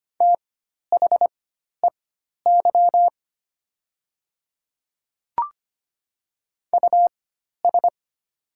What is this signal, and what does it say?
Morse code, heard as a keyed tone, about 25 wpm, spaced as for 12 wpm, 700 Hz. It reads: THEY US
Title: 4x Character Spacing - Top 500 Words - 25wpm